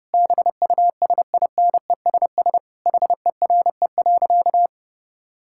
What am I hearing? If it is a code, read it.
Morse code, heard as a keyed tone, about 30 wpm, 700 Hz. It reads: BUSINESS HERE.
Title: Sentences from Top 1000 Words - Review Part 5 - 30wpm